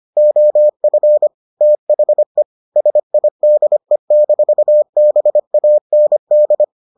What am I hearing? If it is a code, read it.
Morse code, heard as a keyed tone, about 25 wpm, 600 Hz. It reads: OF THE SIDE-BAND